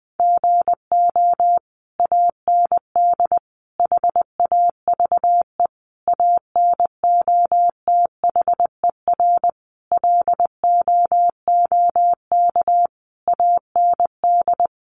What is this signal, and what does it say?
Morse code, heard as a keyed tone, about 20 wpm, 700 Hz. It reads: GO AND HAVE ANOTHER LOOK AND
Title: Sentences from Top 600 Words 20wpm